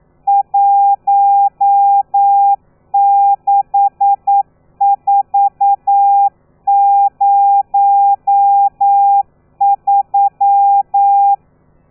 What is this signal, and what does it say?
Morse code, heard as a keyed tone, about 9 wpm, 800 Hz. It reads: 16403
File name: telegrafia / i2tfj / Lez12 40